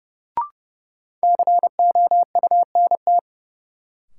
Morse code, heard as a keyed tone, about 32 wpm, 700 Hz. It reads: COUNT
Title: Sentences from Top 1500 Words - New Word Drill Alt - 30wpm